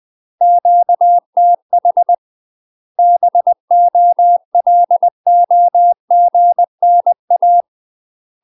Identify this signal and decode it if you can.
Morse code, heard as a keyed tone, about 20 wpm, 700 Hz. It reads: QTH BOLOGNA
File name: UK MASTER QSOs TRAINING/20WPM